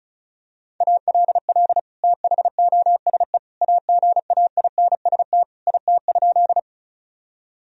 Morse code, heard as a keyed tone, about 36 wpm, 700 Hz. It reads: ALL THOSE AGAINST IT?
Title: Sentences from Top 1000 Words - Review Part 1 - 35wpm